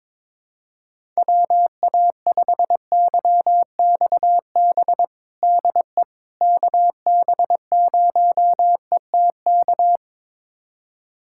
Morse code, heard as a keyed tone, about 22 wpm, 700 Hz. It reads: WA5YXB DE KB0ETK
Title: USA Callsign DE Callsign - 6 Characters - 22wpm